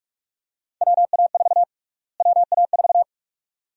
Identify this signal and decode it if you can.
Morse code, heard as a keyed tone, about 46 wpm, 700 Hz. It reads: WA4 WA4
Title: USA Call Sign Prefixes - Repeated 2x - 45wpm